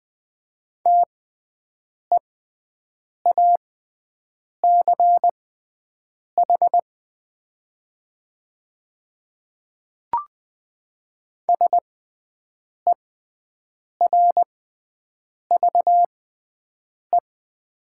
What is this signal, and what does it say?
Morse code, heard as a keyed tone, about 20 wpm, spaced as for 7 wpm, 700 Hz. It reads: TEACH SERVE
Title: 6x Character Spacing - Top 500 Words - 20wpm